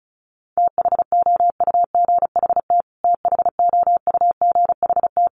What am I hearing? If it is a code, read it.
Morse code, heard as a keyed tone, about 34 wpm, 700 Hz. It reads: THOUGHT THOUGHT
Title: Top 500 Words Repeated 35wpm